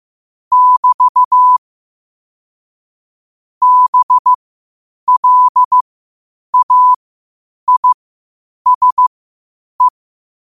Morse code, heard as a keyed tone, about 15 wpm, spaced as for 8 wpm, 1000 Hz. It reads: <BT> BLAISE
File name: qotdfiles / 8wpm